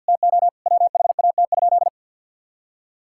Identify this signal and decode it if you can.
Morse code, heard as a keyed tone, about 50 wpm, 700 Hz. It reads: TO WHAT?
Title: Sentences from Top 1000 Words - Review Part 6 - 50wpm